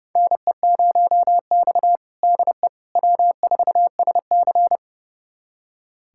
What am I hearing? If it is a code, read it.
Morse code, heard as a keyed tone, about 30 wpm, 700 Hz. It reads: NE0X DE W4SC
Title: USA Callsign DE Callsign - 4 Characters - 30wpm